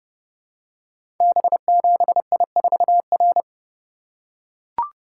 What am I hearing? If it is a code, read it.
Morse code, heard as a keyed tone, about 30 wpm, 700 Hz. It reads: B7I4R
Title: Code Groups - 5 Letter:Numbers - 30wpm